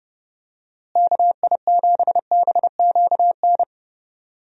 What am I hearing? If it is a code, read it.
Morse code, heard as a keyed tone, about 31 wpm, 700 Hz. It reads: KI7BQN
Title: USA Call Signs - 6 Characters - Repeated 3x - 30wpm